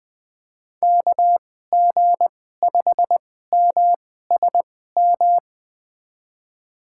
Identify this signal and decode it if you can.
Morse code, heard as a keyed tone, about 20 wpm, spaced as for 14 wpm, 700 Hz. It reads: KG5MSM
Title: USA Call Signs - 6 Characters - 2x Character Spacing - 20wpm